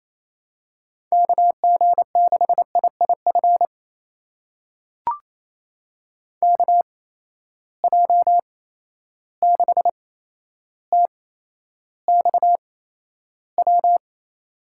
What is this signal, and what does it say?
Morse code, heard as a keyed tone, about 28 wpm, spaced as for 12 wpm, 700 Hz. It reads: KG6IIF KJ6TXW
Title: USA Call Signs - 6 Characters - 8x Character Spacing - 28wpm